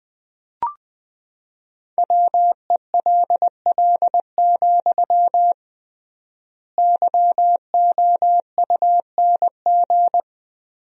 Morse code, heard as a keyed tone, about 21 wpm, 700 Hz. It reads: WELL, YOUNG
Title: 3x Word-Spacing - Sentences from Top 400 Words - 20wpm